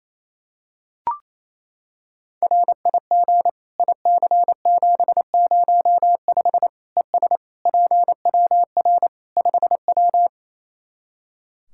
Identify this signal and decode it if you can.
Morse code, heard as a keyed tone, about 28 wpm, 700 Hz. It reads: RIG IC705 ES PWR 5W
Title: Practice QSO - Exchange 2 - Info - 28wpm